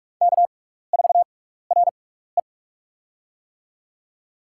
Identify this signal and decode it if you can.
Morse code, heard as a keyed tone, about 46 wpm, spaced as for 16 wpm, 700 Hz. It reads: K4RE
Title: USA Call Signs - 6x Character Spacing - 45wpm